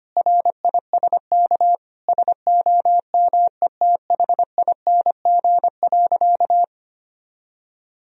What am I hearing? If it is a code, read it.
Morse code, heard as a keyed tone, about 25 wpm, 700 Hz. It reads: RISK SOMETHING.
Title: Sentences from Top 1000 Words - Review Part 4 - 25wpm